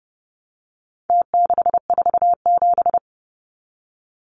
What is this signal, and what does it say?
Morse code, heard as a keyed tone, about 30 wpm, 700 Hz. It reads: T647
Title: Numbers - Contest Style - 4-Digit 30wpm